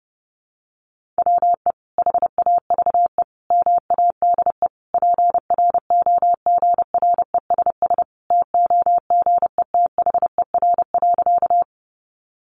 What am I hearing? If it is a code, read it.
Morse code, heard as a keyed tone, about 30 wpm, 700 Hz. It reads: WE HAVE MADE PROGRESS TOGETHER.
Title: Sentences from Top 1400 Words 30wpm